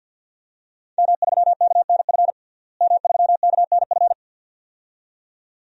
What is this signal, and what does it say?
Morse code, heard as a keyed tone, about 47 wpm, 700 Hz. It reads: M3KNF M3KNF
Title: DXCC-100 Call Signs - 5 Characters - Repeated 2x - 50wpm